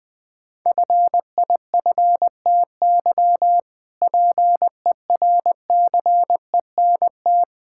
Morse code, heard as a keyed tone, about 21 wpm, 700 Hz. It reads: FIFTY PERCENT
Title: Sentences with 3-Word Phrases - 20wpm